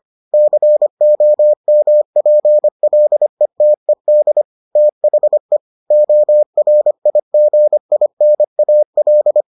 Morse code, heard as a keyed tone, about 25 wpm, 600 Hz. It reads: COMPLETED THE ORIGINAL